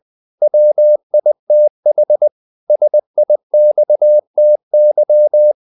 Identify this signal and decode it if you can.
Morse code, heard as a keyed tone, about 20 wpm, 600 Hz. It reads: WITH SIXTY
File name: CW / Practice / If20